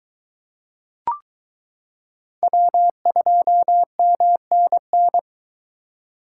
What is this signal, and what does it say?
Morse code, heard as a keyed tone, about 22 wpm, 700 Hz. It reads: W2MNN